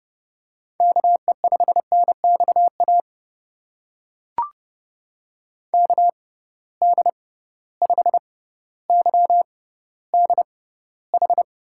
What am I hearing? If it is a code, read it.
Morse code, heard as a keyed tone, about 30 wpm, spaced as for 15 wpm, 700 Hz. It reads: KE5NXA KD5YDH
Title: USA Call Signs - 6 Characters - 6x Character Spacing - 30wpm